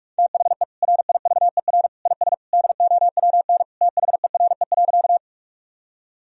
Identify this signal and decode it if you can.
Morse code, heard as a keyed tone, about 46 wpm, 700 Hz. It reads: THE RIVER IS DOWN THERE.
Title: Sentences from Top 1000 Words - Review Part 1 - 45wpm